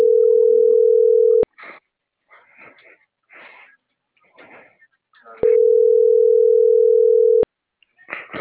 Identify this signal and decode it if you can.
Ringback tone (North America), 440 + 480 Hz.
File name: CENTRAL MABE / out noemi.gutierrez